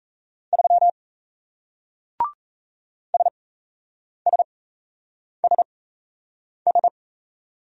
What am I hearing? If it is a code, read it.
Morse code, heard as a keyed tone, about 38 wpm, spaced as for 8 wpm, 700 Hz. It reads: 3 SSSS